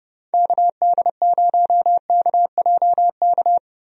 Morse code, heard as a keyed tone, about 32 wpm, 700 Hz. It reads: KD0KJK
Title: USA Call Signs - 6 Characters - Repeated 3x - 30wpm